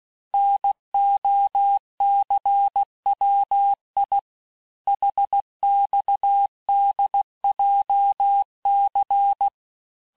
Morse code, heard as a keyed tone, about 16 wpm, 800 Hz. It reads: NOCWI HXDJC
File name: cours-du-lundi files / Cours-1586-texte-chiffre-vitesse-15mpm